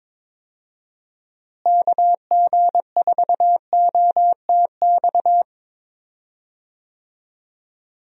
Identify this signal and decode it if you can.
Morse code, heard as a keyed tone, about 21 wpm, 700 Hz. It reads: KG4OTX
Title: USA Call Signs - 6 Characters - Repeated 3x - 22wpm